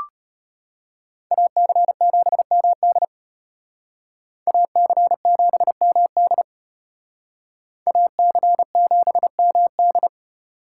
Morse code, heard as a keyed tone, about 33 wpm, 700 Hz. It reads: AC7MD AC7MD AC7MD